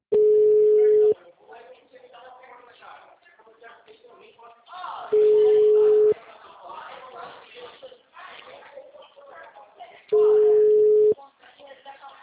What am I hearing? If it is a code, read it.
Ringback tone (Europe, 425 Hz), with speech.
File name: out- -2017- - - .82